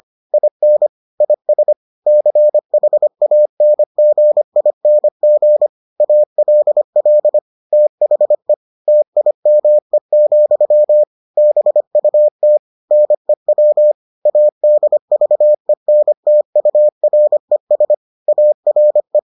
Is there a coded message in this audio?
Yes: Morse code from 0.3 to 19.2 s, heard as a keyed tone, about 25 wpm, 600 Hz. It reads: IN IS CHANGING ALL THE TIME, BUT NEW ADVENTURES ARE